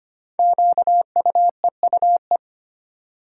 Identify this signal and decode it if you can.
Morse code, heard as a keyed tone, about 25 wpm, 700 Hz. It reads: QUEUE
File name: 5 Letter Words - First 3 Letters Given - 25wpm